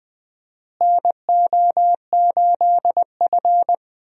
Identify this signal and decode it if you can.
Morse code, heard as a keyed tone, about 20 wpm, 700 Hz. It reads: NO8F
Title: USA Call Signs - 4 Characters - Repeated 3x - 20wpm